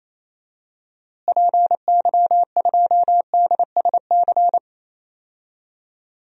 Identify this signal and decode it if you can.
Morse code, heard as a keyed tone, about 28 wpm, 700 Hz. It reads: PY2DSC
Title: DXCC-100 Call Signs - 6 Characters - Repeated 2x - 28wpm